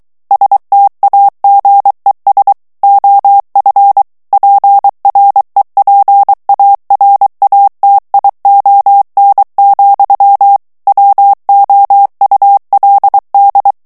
Morse code, heard as a keyed tone, about 23 wpm, 800 Hz. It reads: STAGES OF PREPARATION, WOULD
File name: ANGLAIS texte clair 3 vitesse 22 mots minute